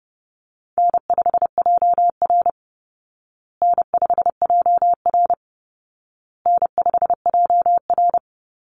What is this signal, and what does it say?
Morse code, heard as a keyed tone, about 30 wpm, 700 Hz. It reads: N5JR N5JR N5JR
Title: Call Signs Repeated 2x - 30wpm